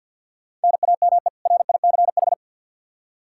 Morse code, heard as a keyed tone, about 50 wpm, 700 Hz. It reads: NAME RICH
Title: K1USN SST QSO Practice 50wpm